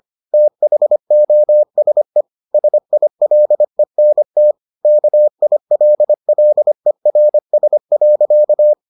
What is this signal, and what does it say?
Morse code, heard as a keyed tone, about 25 wpm, 600 Hz. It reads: THOSE SILENT KILLERS.